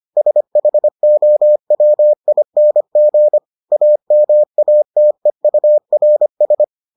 Morse code, heard as a keyed tone, about 25 wpm, 600 Hz. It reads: SHOWING AMATEURS